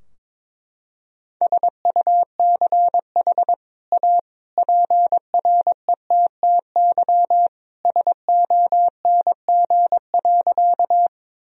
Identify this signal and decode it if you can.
Morse code, heard as a keyed tone, about 21 wpm, 700 Hz. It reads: SUCH A PRETTY SONG.
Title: Sentences from Top 1000 Words - Review Part 2 - 22wpm